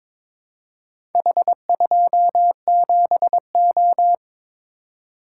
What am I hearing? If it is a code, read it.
Morse code, heard as a keyed tone, about 22 wpm, 700 Hz. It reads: H27O
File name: practice / 22wpm / Four Letter-Number Combinations - 22wpm